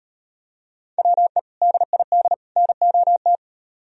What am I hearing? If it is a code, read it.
Morse code, heard as a keyed tone, about 38 wpm, 700 Hz. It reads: WE DID NOT